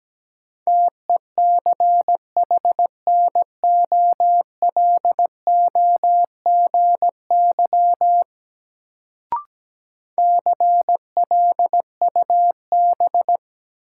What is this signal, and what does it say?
Morse code, heard as a keyed tone, about 17 wpm, 700 Hz. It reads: TECHNOLOGY CLUB
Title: Sentences from Top 1200 Words - New Word Drill Alt - 17wpm